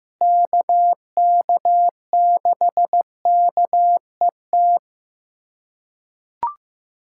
Morse code, heard as a keyed tone, about 15 wpm, 700 Hz. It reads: KK6KET
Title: USA Callsign DE Callsign - 6 Characters - 15wpm